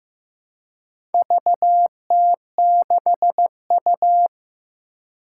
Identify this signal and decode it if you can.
Morse code, heard as a keyed tone, about 15 wpm, 700 Hz. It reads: VT6U
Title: Four Challenging Letter-Number Combinations 15wpm